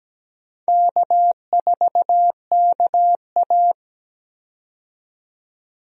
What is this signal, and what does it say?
Morse code, heard as a keyed tone, about 17 wpm, 700 Hz. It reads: K4KA
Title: USA Call Signs - 4 Characters - 17wpm